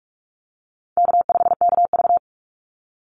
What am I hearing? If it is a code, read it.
Morse code, heard as a keyed tone, about 44 wpm, 700 Hz. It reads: K5KV
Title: Call Signs Repeated 3x - 45wpm